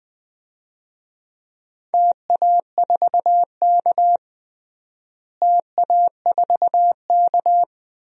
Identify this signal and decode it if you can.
Morse code, heard as a keyed tone, about 19 wpm, 700 Hz. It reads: TA4K TA4K
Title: DXCC-100 Call Signs - 4 Characters - Repeated 2x - 20wpm